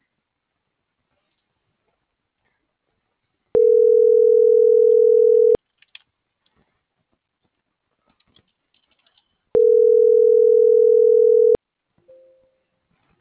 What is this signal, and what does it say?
Ringback tone (North America), 440 + 480 Hz.